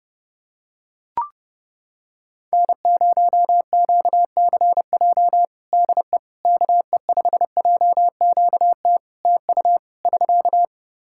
Morse code, heard as a keyed tone, about 30 wpm, 700 Hz. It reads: N0QCJ DE KE5JQT TU <SK>
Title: Practice QSO - Exchange 4 - Ending - 30wpm